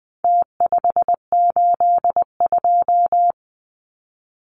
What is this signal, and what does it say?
Morse code, heard as a keyed tone, about 20 wpm, 700 Hz. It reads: T582
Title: Numbers - Contest Style - 4-Digit 20wpm